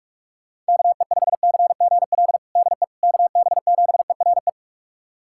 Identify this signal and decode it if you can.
Morse code, heard as a keyed tone, about 41 wpm, 700 Hz. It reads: KE5CGL DE KB7ERE
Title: USA Callsign DE Callsign - 6 Characters - 45wpm